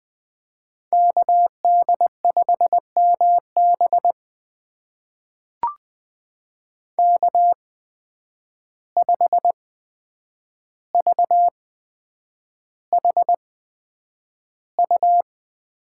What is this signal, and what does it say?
Morse code, heard as a keyed tone, about 20 wpm, spaced as for 9 wpm, 700 Hz. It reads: KD5MB K5VHU
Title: USA Call Signs - 5 Characters - 8x Character Spacing - 20wpm